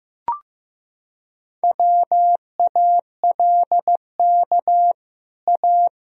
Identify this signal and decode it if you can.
Morse code, heard as a keyed tone, about 14 wpm, 700 Hz. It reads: WALK A
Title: Sentences from Top 1000 Words - Review Part 1 - 15wpm